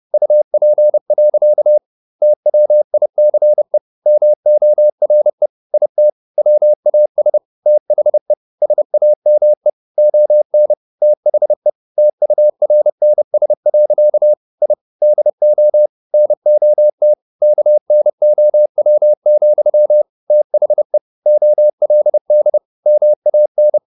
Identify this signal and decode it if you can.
Morse code, heard as a keyed tone, about 30 wpm, 600 Hz. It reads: UP. TWICE MORE IT WAS THE SAME ON THE TURNS. I DO NOT KNOW, THE OLD MAN